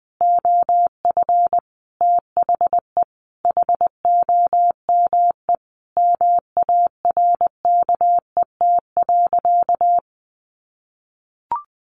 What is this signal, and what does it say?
Morse code, heard as a keyed tone, about 20 wpm, 700 Hz. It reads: OF THE HOME MARKET.
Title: Sentences from Top 1400 Words 20wpm